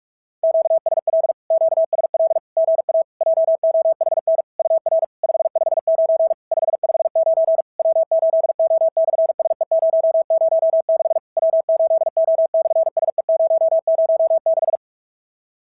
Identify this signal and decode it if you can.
Morse code, heard as a keyed tone, about 51 wpm, 650 Hz. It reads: QSL QSL GA JOHN UR 559 559 W8O/SE006 W8O/SE006
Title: SOTA QSO Practice 45wpm